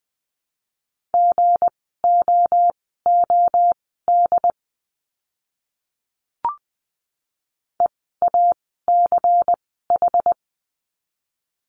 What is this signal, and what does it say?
Morse code, heard as a keyed tone, about 20 wpm, spaced as for 15 wpm, 700 Hz. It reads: GOOD EACH
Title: 206 Lesson 33f - Letter j - Words 20wpm